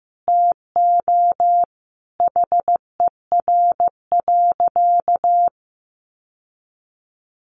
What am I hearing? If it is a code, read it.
Morse code, heard as a keyed tone, about 15 wpm, 700 Hz. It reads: TO HER.
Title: Sentences from Top 1100 Words 15wpm